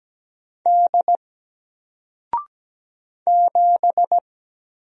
Morse code, heard as a keyed tone, about 17 wpm, 700 Hz. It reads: D 7
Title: Single Character (Letter:Number:Punctuation:Prosign) 17wpm